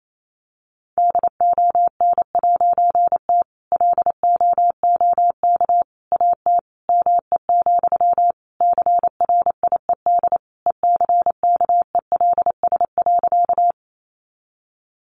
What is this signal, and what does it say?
Morse code, heard as a keyed tone, about 28 wpm, 700 Hz. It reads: DON'T LOOK AT ME, CRIED ECKELS.
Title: A Sound Of Thunder 28wpm